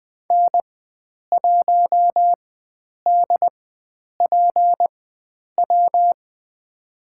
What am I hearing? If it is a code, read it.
Morse code, heard as a keyed tone, about 20 wpm, spaced as for 9 wpm, 700 Hz. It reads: N1DPW
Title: USA Call Signs - 4x Character Spacing - 20wpm